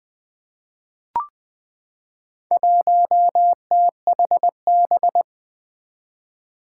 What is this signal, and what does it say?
Morse code, heard as a keyed tone, about 20 wpm, 700 Hz. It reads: 1THB
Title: Four Challenging Letter-Number Combinations 20wpm